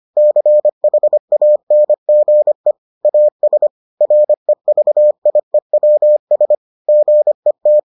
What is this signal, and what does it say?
Morse code, heard as a keyed tone, about 25 wpm, 600 Hz. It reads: CHANGE AS REVIEWS GET